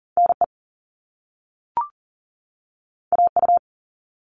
Morse code, heard as a keyed tone, about 41 wpm, 700 Hz. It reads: NE AU
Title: All Cut Numbers - 2-Digit 40wpm